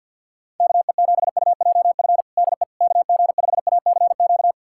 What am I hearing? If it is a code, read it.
Morse code, heard as a keyed tone, about 48 wpm, 700 Hz. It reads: KE7UJF DE KG5ACQ
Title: USA Callsign DE Callsign - 6 Characters - 50wpm